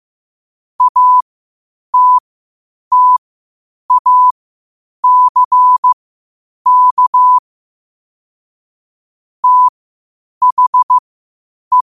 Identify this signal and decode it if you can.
Morse code, heard as a keyed tone, about 15 wpm, spaced as for 8 wpm, 1000 Hz. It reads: ATTACK THE
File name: qotdfiles / 8wpm